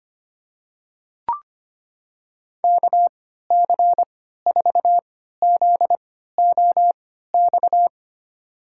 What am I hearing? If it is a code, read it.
Morse code, heard as a keyed tone, about 25 wpm, spaced as for 14 wpm, 700 Hz. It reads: KC4ZOX
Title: USA Call Signs - 6 Characters - 3x Character Spacing - 25wpm